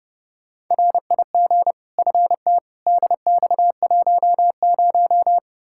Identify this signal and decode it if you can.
Morse code, heard as a keyed tone, about 30 wpm, 700 Hz. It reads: RIG FT DX10
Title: Practice QSO - Exchange 2 - Info - 30wpm